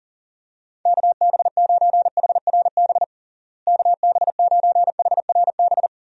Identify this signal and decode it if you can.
Morse code, heard as a keyed tone, about 40 wpm, 700 Hz. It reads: KB9HRB KB9HRB
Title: USA Call Signs - 6 Characters - Repeated 2x - 40wpm